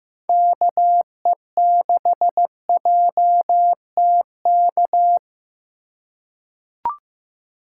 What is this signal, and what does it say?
Morse code, heard as a keyed tone, about 15 wpm, 700 Hz. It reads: KE6JTK
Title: USA Call Signs - 6 Characters - 2x Character Spacing - 15wpm